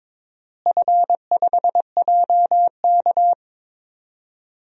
Morse code, heard as a keyed tone, about 22 wpm, 700 Hz. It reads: F5JK
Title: DXCC-100 Call Signs - 4 Characters - Repeated 3x - 22wpm